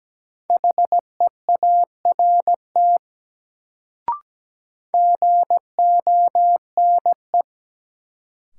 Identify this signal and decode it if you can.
Morse code, heard as a keyed tone, about 17 wpm, 700 Hz. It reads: HEART GONE
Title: Sentences from Top 1200 Words - New Word Drill Alt - 17wpm